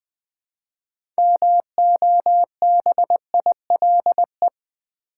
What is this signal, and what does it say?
Morse code, heard as a keyed tone, about 20 wpm, 700 Hz. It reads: MOBILE
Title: 6 Letter Words - First 3 Letters Given - 20wpm